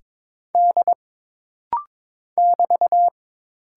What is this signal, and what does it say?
Morse code, heard as a keyed tone, about 22 wpm, 700 Hz. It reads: D <BT>
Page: This rapid-fire timing is geared to push your ICR (Instant Character Recognition) skills.